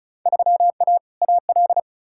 Morse code, heard as a keyed tone, about 36 wpm, 700 Hz. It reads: 3A AL
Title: Field Day Exchanges - 35wpm